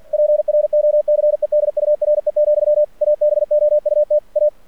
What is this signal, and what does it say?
Morse code, heard as a keyed tone, about 45 wpm, 600 Hz. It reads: PROCEDURE, ABOUT A
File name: archived / CRSnet